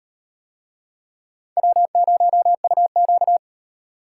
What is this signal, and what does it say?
Morse code, heard as a keyed tone, about 36 wpm, 700 Hz. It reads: W0UQ